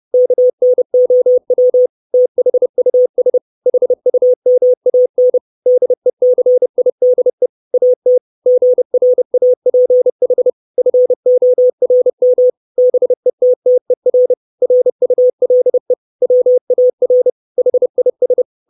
Morse code, heard as a keyed tone, about 30 wpm, 500 Hz. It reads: KNOW THUS HUMAN DECIDE AT GRAPH FORM BETTER RULE WAR HIS
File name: 632 Words 30WPM